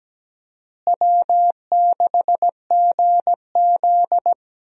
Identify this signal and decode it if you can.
Morse code, heard as a keyed tone, about 17 wpm, 700 Hz. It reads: W6GZ
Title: USA Call Signs - 4 Characters - Repeated 2x - 17wpm